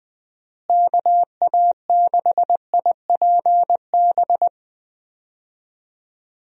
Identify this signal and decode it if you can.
Morse code, heard as a keyed tone, about 20 wpm, 700 Hz. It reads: KA6IPB
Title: USA Call Signs - 6 Characters - Repeated 2x - 20wpm